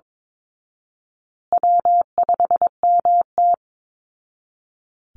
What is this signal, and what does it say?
Morse code, heard as a keyed tone, about 22 wpm, 700 Hz. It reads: W5MT
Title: Easy Call Signs Repeated 2x - 22wpm